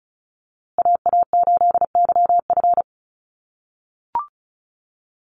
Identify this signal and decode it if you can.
Morse code, heard as a keyed tone, about 34 wpm, 700 Hz. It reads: AA8YF
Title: USA Call Signs Repeated 3x 35wpm